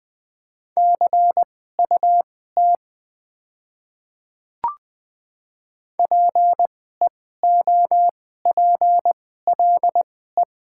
Morse code, heard as a keyed tone, about 20 wpm, spaced as for 14 wpm, 700 Hz. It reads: CUT PEOPLE
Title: 2x Character Spacing - Top 300 Words - 20wpm